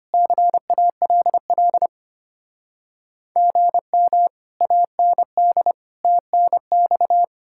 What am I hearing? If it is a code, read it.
Morse code, heard as a keyed tone, about 33 wpm, 700 Hz. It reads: CALL GM AND TNX